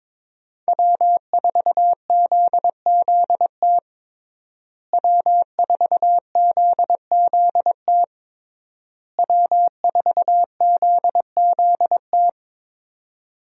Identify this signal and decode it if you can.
Morse code, heard as a keyed tone, about 22 wpm, 700 Hz. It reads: W4ZZT W4ZZT W4ZZT
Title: USA Call Signs - 5 Characters - Repeated 3x - 22wpm